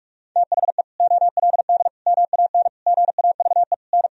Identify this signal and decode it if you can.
Morse code, heard as a keyed tone, about 46 wpm, 700 Hz. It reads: THE OLD MAN GAVE N
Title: Sentences from Top 500 Words and Abbreviations 45wpm